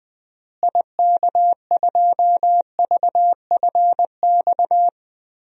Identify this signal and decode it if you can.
Morse code, heard as a keyed tone, about 21 wpm, 700 Hz. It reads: IK2VFX
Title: DXCC-100 Call Signs - 6 Characters - Repeated 2x - 20wpm